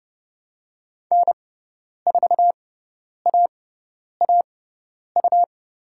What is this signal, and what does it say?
Morse code, heard as a keyed tone, about 30 wpm, spaced as for 10 wpm, 690 Hz. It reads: N4AAU
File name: Lesson 18g - Letter m - Call Signs 30wpm